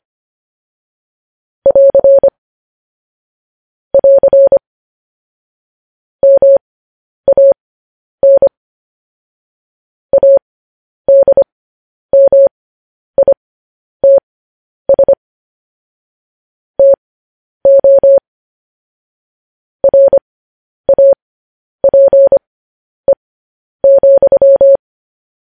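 Morse code, heard as a keyed tone, about 25 wpm, spaced as for 10 wpm, 550 Hz. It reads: <AR> <AR> MAN ADMITS TO RAPE,
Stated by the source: Morse news headlines practice at 10.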